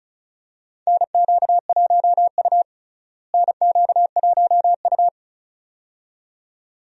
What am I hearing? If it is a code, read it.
Morse code, heard as a keyed tone, about 34 wpm, 700 Hz. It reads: NQ1U NQ1U
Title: USA Call Signs - 4 Characters - Repeated 2x - 35wpm